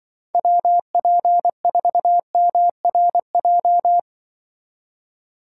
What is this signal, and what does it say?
Morse code, heard as a keyed tone, about 23 wpm, 700 Hz. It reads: WP4MRJ
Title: Speed Racing - USA 2x3 Call Signs 20wpm